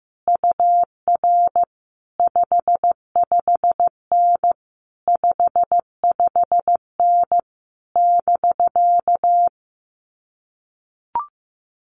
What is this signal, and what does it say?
Morse code, heard as a keyed tone, about 15 wpm, 700 Hz. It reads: UR 55N 55N <BK>
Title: Common QSO Phrases Encore 15wpm